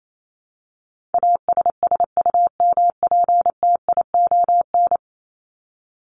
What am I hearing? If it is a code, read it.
Morse code, heard as a keyed tone, about 28 wpm, 700 Hz. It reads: ASSUMPTION
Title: Top 1751-2000 Words 28wpm